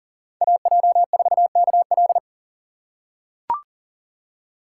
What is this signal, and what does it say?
Morse code, heard as a keyed tone, about 40 wpm, 700 Hz. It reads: AJ4KL
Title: USA Call Signs - 3x Character Spacing - 40wpm